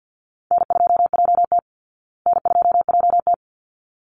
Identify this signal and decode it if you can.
Morse code, heard as a keyed tone, about 50 wpm, 700 Hz. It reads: N2JT N2JT
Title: Easy Call Signs Repeated 2x - 50wpm